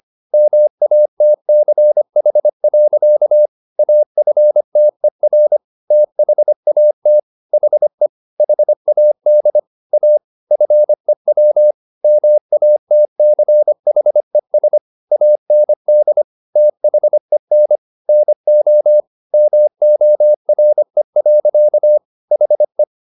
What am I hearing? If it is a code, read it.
Morse code, heard as a keyed tone, about 25 wpm, 600 Hz. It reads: MATCH. AFTER THAT HE HAD A FEW MATCHES AND THEN NO MORE. HE